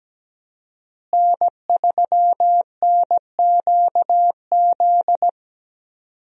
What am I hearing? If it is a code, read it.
Morse code, heard as a keyed tone, about 17 wpm, 700 Hz. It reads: N3NQZ
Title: USA Call Signs - 5 Characters - Repeated 3x - 17wpm